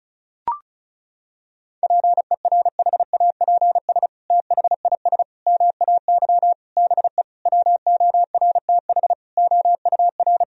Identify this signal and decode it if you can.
Morse code, heard as a keyed tone, about 36 wpm, 700 Hz. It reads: PERHAPS THIS MAY BE WORTH OUR
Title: Sentences from Top 1000 Words - Review Part 2 - 35wpm